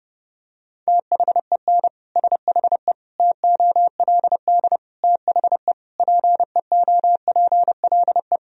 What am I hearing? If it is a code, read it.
Morse code, heard as a keyed tone, about 30 wpm, 700 Hz. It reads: THEN SHE TOLD THE PEOPLE
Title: Sentences from Top 1000 Words - Review Part 2 - 30wpm